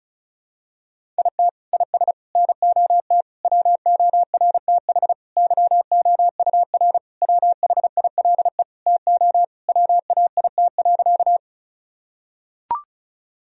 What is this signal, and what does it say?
Morse code, heard as a keyed tone, about 34 wpm, 700 Hz. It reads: IT IS NOT WORTH YOUR WHILE TO WAIT.
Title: Sentences from Top 1000 Words - Review Part 1 - 35wpm